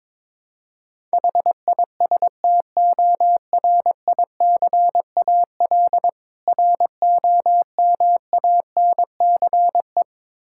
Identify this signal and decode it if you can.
Morse code, heard as a keyed tone, about 22 wpm, 690 Hz. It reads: HISTORICAL ROMANCE
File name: Difficult Words with Hints - 22wpm